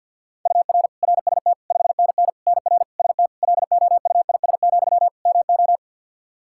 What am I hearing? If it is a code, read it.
Morse code, heard as a keyed tone, about 52 wpm, 700 Hz. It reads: UR RST 5NN NR ST LOUIS, MO
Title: Practice QSO - Exchange 1 - Beginning - 50wpm